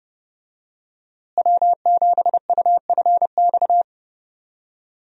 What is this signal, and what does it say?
Morse code, heard as a keyed tone, about 32 wpm, 700 Hz. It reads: W7UFX
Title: USA Call Signs - 5 Characters - Repeated 2x - 30wpm